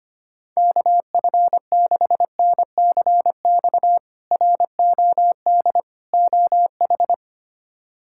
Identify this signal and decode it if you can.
Morse code, heard as a keyed tone, about 25 wpm, 700 Hz. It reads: KF6NCX ROD OH
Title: K1USN SST QSO Practice - 25wpm